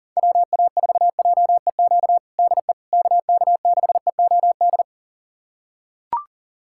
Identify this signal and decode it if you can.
Morse code, heard as a keyed tone, about 40 wpm, 700 Hz. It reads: WA4JEQ DE KK6EOD
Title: USA Callsign DE Callsign 40wpm